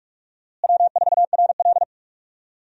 Morse code, heard as a keyed tone, about 47 wpm, 700 Hz. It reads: W4RL
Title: USA Call Signs - 4 Characters - Repeated 2x - 45wpm